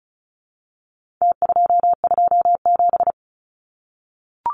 Morse code, heard as a keyed tone, about 34 wpm, 700 Hz. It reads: T227
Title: Numbers - Contest Style - 4-Digit 35wpm